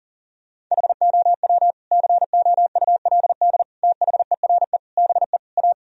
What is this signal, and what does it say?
Morse code, heard as a keyed tone, about 40 wpm, 700 Hz. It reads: HOW COULD THERE BE A
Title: Sentences from Top 1000 Words - Review Part 3 - 40wpm